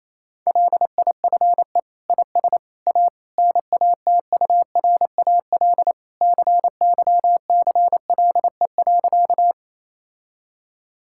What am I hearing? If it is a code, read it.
Morse code, heard as a keyed tone, about 27 wpm, 700 Hz. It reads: LIFE IS A NATURAL CYCLE.
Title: Sentences from Top 1000 Words - Review Part 6 - 28wpm